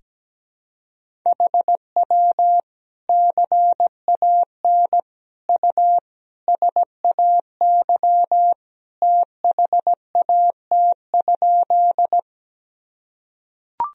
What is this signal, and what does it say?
Morse code, heard as a keyed tone, about 17 wpm, 700 Hz. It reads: HW CAN U SAY THAT?
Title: Sentences from Top 200 Words and Abbreviations 17wpm